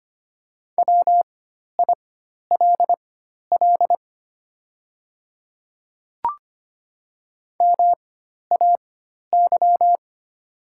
Morse code, heard as a keyed tone, about 25 wpm, spaced as for 12 wpm, 700 Hz. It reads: WILL MAY
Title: 4x Character Spacing - Top 300 Words - 25wpm